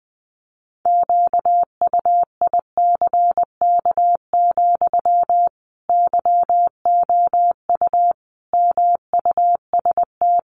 Morse code, heard as a keyed tone, about 20 wpm, 700 Hz. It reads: QUICK, YOU MUST
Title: Sentences from Top 500 Words 20wpm